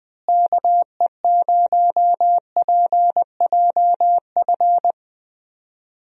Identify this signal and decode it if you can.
Morse code, heard as a keyed tone, about 20 wpm, 700 Hz. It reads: KE0PJF
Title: USA Call Signs - 6 Characters - Repeated 3x - 20wpm